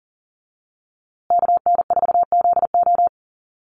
Morse code, heard as a keyed tone, about 40 wpm, 700 Hz. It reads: KN4ZO
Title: USA Call Signs Repeated 2x 40wpm